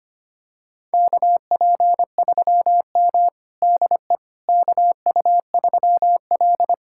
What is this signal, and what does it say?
Morse code, heard as a keyed tone, about 25 wpm, 700 Hz. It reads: KP3M DE KU3L
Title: USA Callsign DE Callsign - 4 Characters - 25wpm